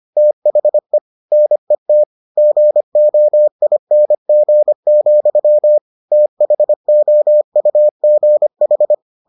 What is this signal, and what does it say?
Morse code, heard as a keyed tone, about 25 wpm, 600 Hz. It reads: THE NET GOING, THOUGH